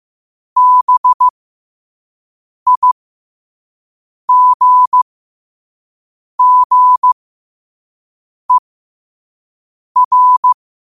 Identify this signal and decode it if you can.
Morse code, heard as a keyed tone, about 15 wpm, spaced as for 5 wpm, 1000 Hz. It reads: BIGGER
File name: qotdfiles / 5wpm